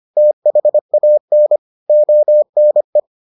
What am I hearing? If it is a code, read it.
Morse code, heard as a keyed tone, about 25 wpm, 600 Hz. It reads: THAN ONE